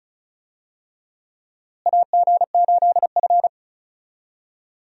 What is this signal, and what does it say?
Morse code, heard as a keyed tone, about 36 wpm, 700 Hz. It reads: AG8F
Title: USA Call Signs - 4 Characters - Repeated 3x - 35wpm